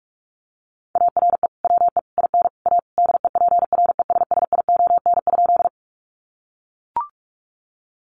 Morse code, heard as a keyed tone, about 44 wpm, 700 Hz. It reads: ARE WE IN A DEPRESSION?
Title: Sentences from Top 1300 Words 45wpm